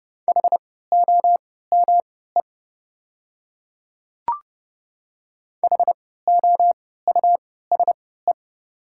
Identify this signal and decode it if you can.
Morse code, heard as a keyed tone, about 30 wpm, spaced as for 17 wpm, 690 Hz. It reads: HOME HOUSE
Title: 3x Character Spacing - Top 200 Words - 30wpm